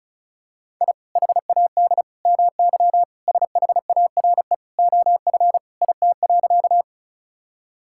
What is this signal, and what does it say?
Morse code, heard as a keyed tone, about 36 wpm, 700 Hz. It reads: I HAD MY SHARE OF IT.
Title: Sentences from Top 1000 Words - Review Part 6 - 35wpm